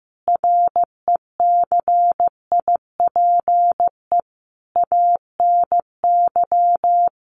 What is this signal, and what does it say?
Morse code, heard as a keyed tone, about 14 wpm, 700 Hz. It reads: RECIPE ANY
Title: Sentences from Top 1200 Words 15wpm